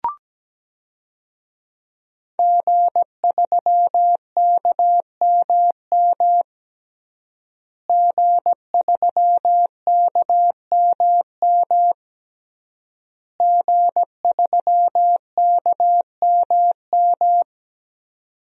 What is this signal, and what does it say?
Morse code, heard as a keyed tone, about 17 wpm, 700 Hz. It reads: G3KMM G3KMM G3KMM
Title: DXCC-100 Call Signs - 5 Characters - Repeated 3x - 17wpm